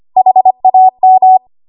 Morse code, heard as a keyed tone, about 25 wpm, 750 Hz. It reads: HAM
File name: Morse / 25WPM